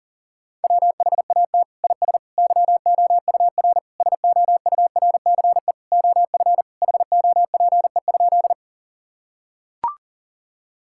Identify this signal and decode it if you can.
Morse code, heard as a keyed tone, about 37 wpm, 700 Hz. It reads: WHAT IS YOUR SOURCE OF HOPE?
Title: Sentences from Top 1000 Words - Review Part 5 - 40wpm